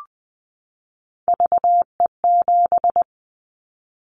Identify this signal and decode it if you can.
Morse code, heard as a keyed tone, about 20 wpm, 700 Hz. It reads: VE7
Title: Callsign Prefixes for Top 100 DXCC - 20wpm